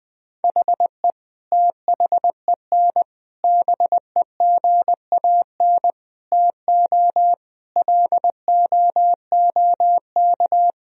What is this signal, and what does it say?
Morse code, heard as a keyed tone, about 20 wpm, 700 Hz. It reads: HE THEN BEGAN TO LOOK
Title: Sentences from Top 1000 Words - Review Part 3 - 20wpm